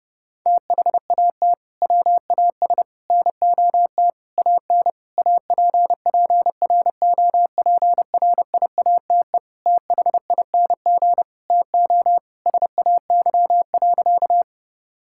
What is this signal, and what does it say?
Morse code, heard as a keyed tone, about 32 wpm, 700 Hz. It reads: THAT WAS NOT AN APPROPRIATE THING TO SAY.
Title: Sentences from Top 1000 Words - Review Part 5 - 30wpm